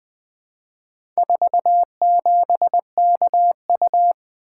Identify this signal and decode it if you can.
Morse code, heard as a keyed tone, about 20 wpm, 700 Hz. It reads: 47KU
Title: Four Letter-Number Combinations - 20wpm